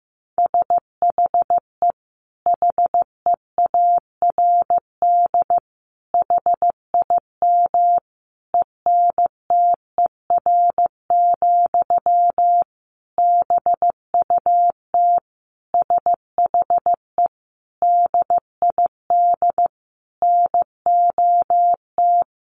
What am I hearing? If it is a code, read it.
Morse code, heard as a keyed tone, about 15 wpm, 700 Hz. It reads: SHE HEARD HIM ENTER, BUT SHE DID NOT
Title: Sentences from Top 1400 Words 15wpm